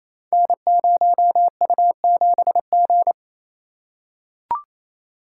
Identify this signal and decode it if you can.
Morse code, heard as a keyed tone, about 27 wpm, 700 Hz. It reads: N0U7G
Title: Code Groups - 5 Letter:Numbers - 28wpm